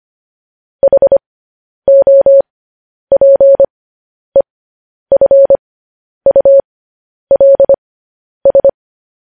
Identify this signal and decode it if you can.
Morse code, heard as a keyed tone, about 25 wpm, spaced as for 10 wpm, 550 Hz. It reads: HOPEFULS
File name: podcast / News headlines-1x-25x10